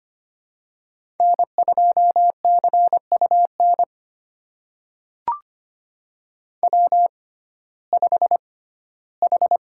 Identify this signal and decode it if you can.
Morse code, heard as a keyed tone, about 25 wpm, spaced as for 15 wpm, 700 Hz. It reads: N2CUN W5H
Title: USA Call Signs - 6x Character Spacing - 25wpm